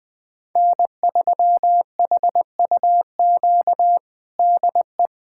Morse code, heard as a keyed tone, about 20 wpm, 700 Hz. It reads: N3HUQ DE
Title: USA Callsign DE Callsign - 5 Characters - 20wpm